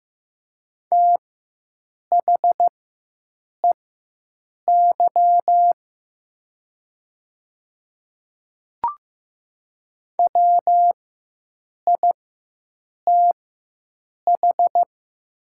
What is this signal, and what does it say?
Morse code, heard as a keyed tone, about 15 wpm, spaced as for 7 wpm, 700 Hz. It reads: THEY WITH
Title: 4x Character Spacing - Top 100 Words - 15wpm